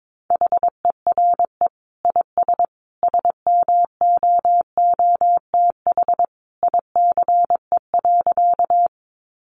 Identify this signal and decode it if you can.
Morse code, heard as a keyed tone, about 22 wpm, 700 Hz. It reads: HERE IS SMOOTH ICE.
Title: Sentences from Top 1400 Words 22wpm